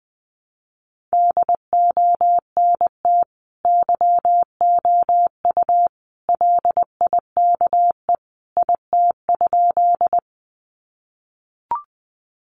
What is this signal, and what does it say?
Morse code, heard as a keyed tone, about 20 wpm, 700 Hz. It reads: DONT YOU LIKE IT?
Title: Sentences from Top 300 Words 20wpm